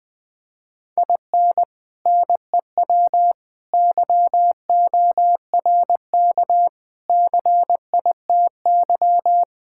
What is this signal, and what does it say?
Morse code, heard as a keyed tone, about 20 wpm, 700 Hz. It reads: IN NEW YORK CITY
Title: Sentences with 4-Word Phrases - 20wpm